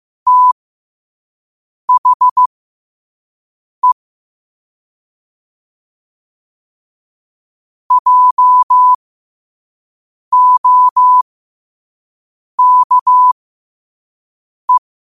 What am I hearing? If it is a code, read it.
Morse code, heard as a keyed tone, about 15 wpm, spaced as for 5 wpm, 1000 Hz. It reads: THE JOKE
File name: qotdfiles / 5wpm